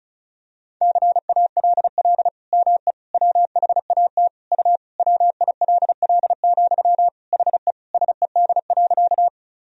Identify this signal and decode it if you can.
Morse code, heard as a keyed tone, about 34 wpm, 700 Hz. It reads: CALL ME WHAT U WILL, HE SED.
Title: Sentences from Top 200 Words and Abbreviations 35wpm